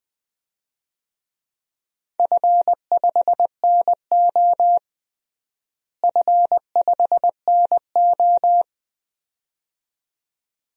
Morse code, heard as a keyed tone, about 21 wpm, 700 Hz. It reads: F5NO F5NO
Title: DXCC-100 Call Signs - 4 Characters - Repeated 2x - 20wpm